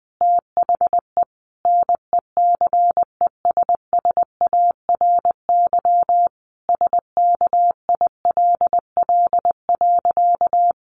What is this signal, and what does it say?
Morse code, heard as a keyed tone, about 20 wpm, 700 Hz. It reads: THE NECESSARY SKILL.
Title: Sentences from Top 1200 Words 20wpm